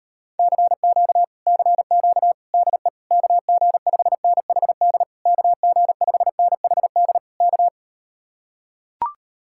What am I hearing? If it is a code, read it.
Morse code, heard as a keyed tone, about 38 wpm, 700 Hz. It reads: CQ CQ DE KG5NHD KG5NHD K